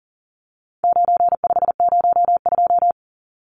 Morse code, heard as a keyed tone, about 40 wpm, 700 Hz. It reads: 9502
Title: Numbers - 4-Digit 40wpm